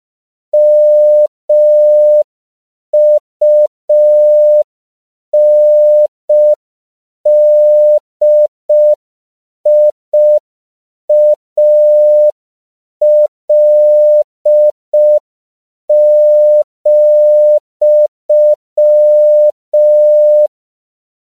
Morse code, heard as a keyed tone, about 5 wpm, 600 Hz. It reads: MUNDIAL,
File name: CW / 5PPM